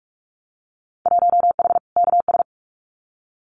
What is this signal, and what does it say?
Morse code, heard as a keyed tone, about 46 wpm, 700 Hz. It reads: 1H KS
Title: Winter Field Day Exchanges Encore 45wpm